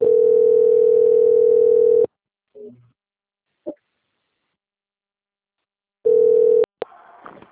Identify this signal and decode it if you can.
Ringback tone (North America), 440 + 480 Hz.
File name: out- - - - - .8468